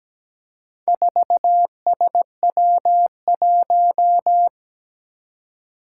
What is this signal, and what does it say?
Morse code, heard as a keyed tone, about 17 wpm, 700 Hz. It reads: 4SW1
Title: Four Challenging Letter-Number Combinations 17wpm